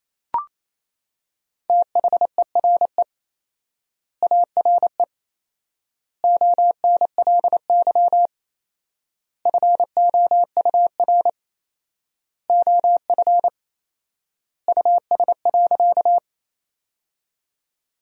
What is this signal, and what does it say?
Morse code, heard as a keyed tone, about 28 wpm, 700 Hz. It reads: THERE ARE ONLY FOUR OF US.
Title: 4x Word-Spacing - Sentences from Top 300 Words - 28wpm